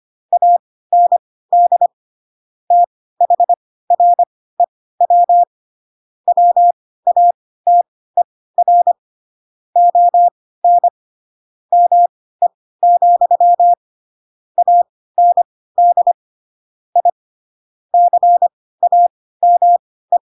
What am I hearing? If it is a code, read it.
Morse code, heard as a keyed tone, about 25 wpm, spaced as for 16 wpm, 700 Hz. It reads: AND THREW WATER ON ME, AND I CAME
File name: Pinocchio / Pinocchio ch7 10 25 ik1hge